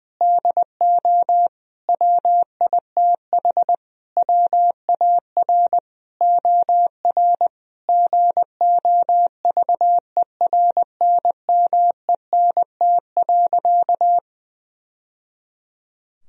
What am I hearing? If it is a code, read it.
Morse code, heard as a keyed tone, about 20 wpm, 700 Hz. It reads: DO WITH WAR OR GOVERNMENT.
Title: Sentences from Top 1000 Words - Review Part 3 - 20wpm